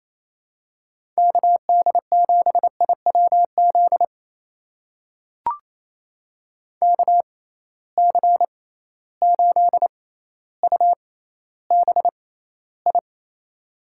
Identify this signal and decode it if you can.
Morse code, heard as a keyed tone, about 28 wpm, spaced as for 14 wpm, 700 Hz. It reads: KD7IWZ KC8UBI
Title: USA Call Signs - 6 Characters - 6x Character Spacing - 28wpm